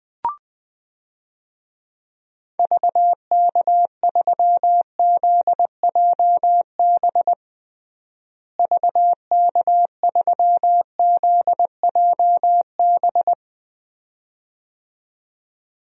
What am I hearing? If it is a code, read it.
Morse code, heard as a keyed tone, about 20 wpm, 700 Hz. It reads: VK3ZJB VK3ZJB
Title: DXCC-100 Call Signs - 6 Characters - Repeated 2x - 20wpm